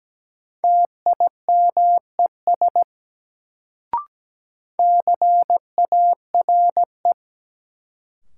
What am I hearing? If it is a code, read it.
Morse code, heard as a keyed tone, about 17 wpm, 700 Hz. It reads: TIMES CARE
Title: Sets of 3 Words - New Word Drill - 17wpm